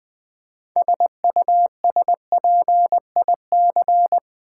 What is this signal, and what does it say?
Morse code, heard as a keyed tone, about 20 wpm, 700 Hz. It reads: SUSPIC
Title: Build A Word - 3 Letters At A Time - 9-Letter Words - 20wpm